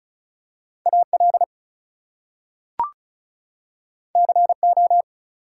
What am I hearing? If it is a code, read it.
Morse code, heard as a keyed tone, about 36 wpm, 700 Hz. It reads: AL CO
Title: US States and Canadian Provinces 35wpm